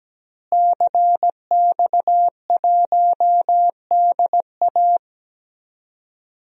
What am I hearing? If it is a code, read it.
Morse code, heard as a keyed tone, about 17 wpm, 700 Hz. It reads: CX1DA
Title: DXCC-100 Call Signs - 5 Characters - Repeated 3x - 17wpm